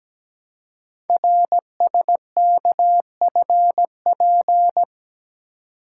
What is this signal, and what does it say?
Morse code, heard as a keyed tone, about 17 wpm, 700 Hz. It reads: RSKFP
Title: Code Groups - 5 Letters - 17wpm